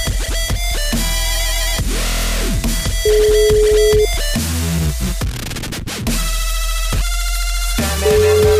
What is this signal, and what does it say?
Ringback tone (Europe, 425 Hz), with noise.